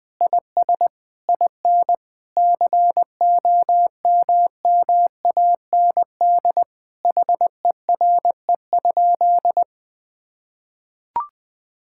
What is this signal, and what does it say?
Morse code, heard as a keyed tone, about 20 wpm, 700 Hz. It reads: IS IN COMMAND HERE?
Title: Sentences from Top 1600 Words 20wpm